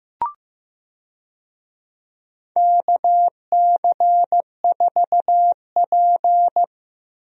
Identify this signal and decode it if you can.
Morse code, heard as a keyed tone, about 15 wpm, 700 Hz. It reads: KC4P
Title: USA Call Signs - 4 Characters - Repeated 2x - 15wpm